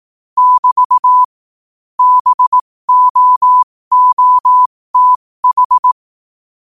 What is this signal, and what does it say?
Morse code, heard as a keyed tone, about 18 wpm, spaced as for 15 wpm, 1000 Hz. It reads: <BT> BOOTH
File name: qotdfiles / 15wpm